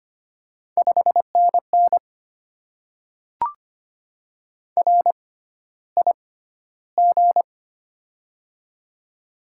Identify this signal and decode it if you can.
Morse code, heard as a keyed tone, about 25 wpm, spaced as for 13 wpm, 700 Hz. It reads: 5NN RIG
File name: Top 20 QSO Elements - 6x Character Spacing - 25wpm